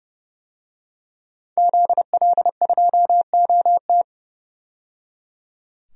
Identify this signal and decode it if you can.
Morse code, heard as a keyed tone, about 31 wpm, 700 Hz. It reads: ZL2OT
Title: DXCC-100 Call Signs - 5 Characters - 30wpm